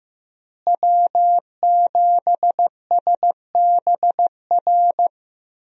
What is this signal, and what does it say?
Morse code, heard as a keyed tone, about 15 wpm, 700 Hz. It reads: W7SBR
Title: USA Call Signs - 5 Characters - Repeated 3x - 15wpm